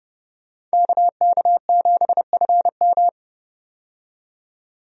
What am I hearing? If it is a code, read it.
Morse code, heard as a keyed tone, about 32 wpm, 700 Hz. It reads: KK7FM
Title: USA Call Signs - 5 Characters - 30wpm